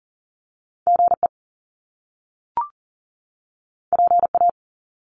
Morse code, heard as a keyed tone, about 39 wpm, 700 Hz. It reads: GE PA
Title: Top Two Letter Combinations in Words 40wpm